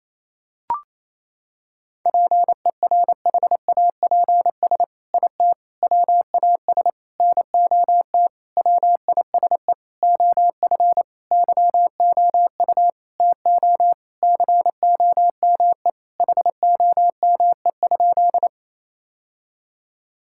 Morse code, heard as a keyed tone, about 28 wpm, 700 Hz. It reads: PERHAPS IT WAS NOT WISE OF YOU TO COME HOME?
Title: Sentences from Top 1600 Words 28wpm